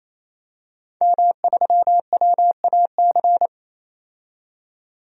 Morse code, heard as a keyed tone, about 28 wpm, 700 Hz. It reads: M3WAC
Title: DXCC-100 Call Signs - 5 Characters - 28wpm